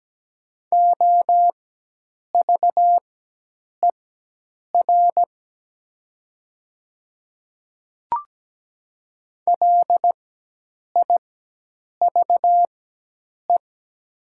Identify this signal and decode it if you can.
Morse code, heard as a keyed tone, about 17 wpm, spaced as for 8 wpm, 700 Hz. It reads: OVER LIVE
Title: 4x Character Spacing - Top 200 Words - 17wpm